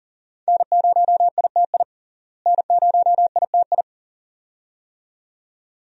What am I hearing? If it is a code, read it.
Morse code, heard as a keyed tone, about 40 wpm, 700 Hz. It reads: N0ITI N0ITI
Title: USA Call Signs - 5 Characters - Repeated 2x - 40wpm